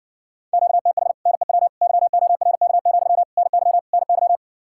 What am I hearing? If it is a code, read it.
Morse code, heard as a keyed tone, about 60 wpm, 700 Hz. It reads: QTH NEW YORK, NY NY